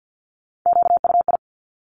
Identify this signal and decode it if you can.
Morse code, heard as a keyed tone, about 50 wpm, 700 Hz. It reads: QUI
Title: Top Three Letter Combinations in Words 50wpm